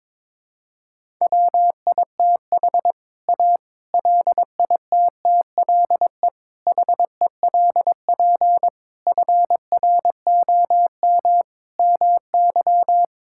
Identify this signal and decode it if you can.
Morse code, heard as a keyed tone, about 22 wpm, 700 Hz. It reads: WITH A LITTLE HELP FROM MY
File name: Sentences with 7-Word Phrases - 22wpm